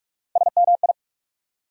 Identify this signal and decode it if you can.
Morse code, heard as a keyed tone, about 47 wpm, 700 Hz. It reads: SMI
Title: USA Call Sign Postfixes - 3 Letters 45wpm